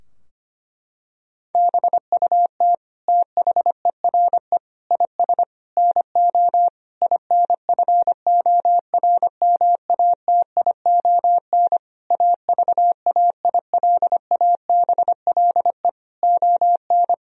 Morse code, heard as a keyed tone, about 25 wpm, 700 Hz. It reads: BUT THERE IS NO INFORMATION AVAILABLE ON
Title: Sentences from Top 1000 Words - Review Part 3 - 25wpm